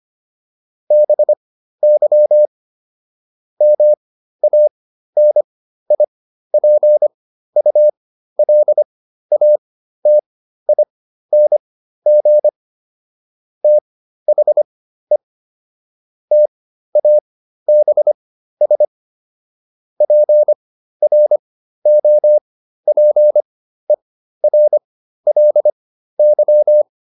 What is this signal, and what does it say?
Morse code, heard as a keyed tone, about 25 wpm, spaced as for 13 wpm, 600 Hz. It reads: BY MANIPULATING THE TABS PROPERLY